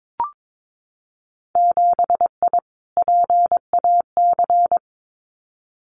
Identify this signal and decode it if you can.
Morse code, heard as a keyed tone, about 22 wpm, 700 Hz. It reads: 7I PAC
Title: Winter Field Day Exchanges Encore 22wpm